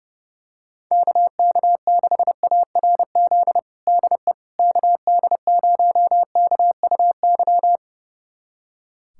Morse code, heard as a keyed tone, about 31 wpm, 700 Hz. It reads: KK6ARZ DE KD0KUY
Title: USA Callsign DE Callsign 30wpm